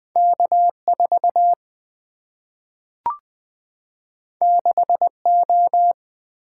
Morse code, heard as a keyed tone, about 20 wpm, 700 Hz. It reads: K 4 6 O
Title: Single-Letter Single-Number 20wpm